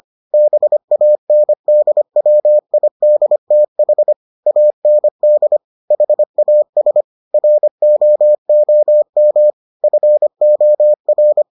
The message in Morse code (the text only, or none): BANDWIDTH AND HAS ROOM FOR